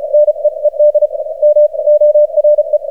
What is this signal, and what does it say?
Morse code, heard as a keyed tone, about 35 wpm, spaced as for 30 wpm, 600 Hz. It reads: NEED MORE
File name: archived / CRSnet